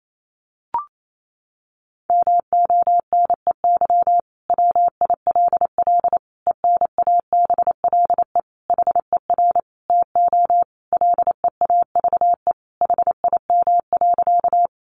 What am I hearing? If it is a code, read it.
Morse code, heard as a keyed tone, about 28 wpm, 700 Hz. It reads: MONEY WILL ENABLE HER TO LEAVE HIM.
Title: Sentences from Top 1500 Words 28wpm